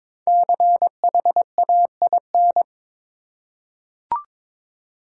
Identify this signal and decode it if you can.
Morse code, heard as a keyed tone, about 22 wpm, 700 Hz. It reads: CHAIN
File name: 5 Letter Words - First 3 Letters Given - 22wpm